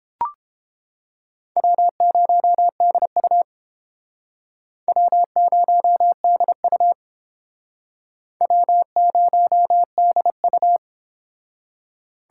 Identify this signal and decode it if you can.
Morse code, heard as a keyed tone, about 32 wpm, 700 Hz. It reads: W0DU W0DU W0DU